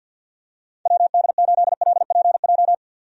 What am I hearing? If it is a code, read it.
Morse code, heard as a keyed tone, about 50 wpm, 700 Hz. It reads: WD8LPJ
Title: USA Call Signs - 6 Characters - 50wpm